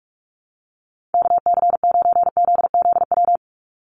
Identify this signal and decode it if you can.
Morse code, heard as a keyed tone, about 46 wpm, 690 Hz. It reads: KC9ZZW
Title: USA Call Signs 45wpm